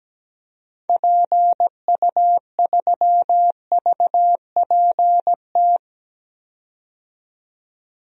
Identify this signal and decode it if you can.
Morse code, heard as a keyed tone, about 17 wpm, 700 Hz. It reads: PU3VPT
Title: DXCC-100 Call Signs - 6 Characters - 17wpm